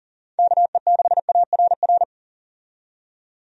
Morse code, heard as a keyed tone, about 40 wpm, 700 Hz. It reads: KE6ARR
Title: USA Call Signs - 6 Characters - 40wpm